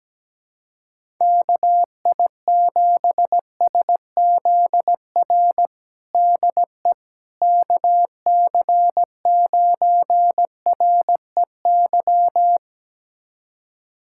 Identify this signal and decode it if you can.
Morse code, heard as a keyed tone, about 17 wpm, 700 Hz. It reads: KI7SZR DE KC9REY
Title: USA Callsign DE Callsign - 6 Characters - 17wpm